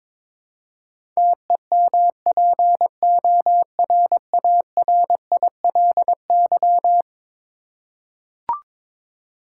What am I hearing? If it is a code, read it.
Morse code, heard as a keyed tone, about 22 wpm, 700 Hz. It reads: TEMPORARILY
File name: practice / 22wpm / Eleven Letter Words - 22wpm